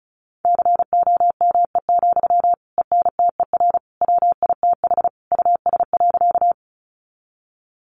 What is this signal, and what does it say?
Morse code, heard as a keyed tone, about 34 wpm, 700 Hz. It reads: COME, ENTER WITH US.
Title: Sentences from Top 1300 Words 35wpm